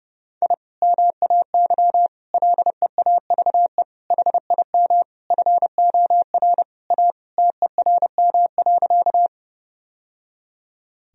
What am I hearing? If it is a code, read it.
Morse code, heard as a keyed tone, about 32 wpm, 700 Hz. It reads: I MAY LEAVE HIM FOR A TERM.
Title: Sentences from Top 1000 Words - Review Part 4 - 30wpm